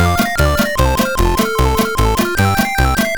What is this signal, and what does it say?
There is no code.